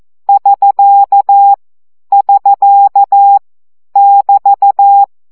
Morse code, heard as a keyed tone, about 14 wpm, 800 Hz. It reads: <SK> <SK> <BT>
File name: CW gyakorlas sk 12 baud